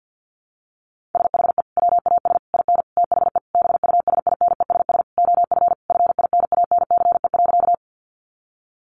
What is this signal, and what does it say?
Morse code, heard as a keyed tone, about 50 wpm, 700 Hz. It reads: SHE WAS IN THE BUSINESS OF FINANCE.
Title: Sentences from Top 1500 Words 50wpm